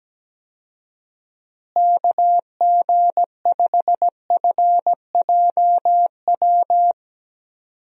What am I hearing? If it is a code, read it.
Morse code, heard as a keyed tone, about 17 wpm, 700 Hz. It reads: KG5FJW
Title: USA Call Signs - 6 Characters - Repeated 2x - 17wpm